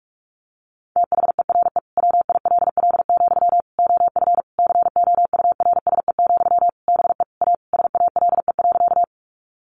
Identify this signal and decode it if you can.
Morse code, heard as a keyed tone, about 44 wpm, 700 Hz. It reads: THERE WILL, OF COURSE, BE A SALE.
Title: Sentences from Top 1400 Words 45wpm